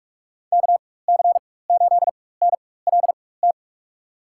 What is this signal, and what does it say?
Morse code, heard as a keyed tone, about 46 wpm, spaced as for 21 wpm, 700 Hz. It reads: KC8NLT
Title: USA Call Signs - 6 Characters - 4x Character Spacing - 45wpm